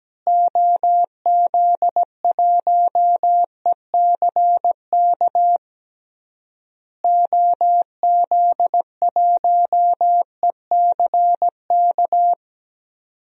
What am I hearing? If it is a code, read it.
Morse code, heard as a keyed tone, about 17 wpm, 700 Hz. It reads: OZ1ECK OZ1ECK
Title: DXCC-100 Call Signs - 6 Characters - Repeated 3x - 17wpm